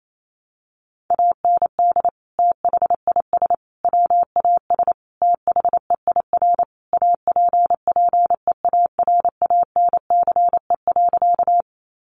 Morse code, heard as a keyed tone, about 28 wpm, 700 Hz. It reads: AND THIS WAS THEIR APPEARANCE.
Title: Sentences from Top 1100 Words 28wpm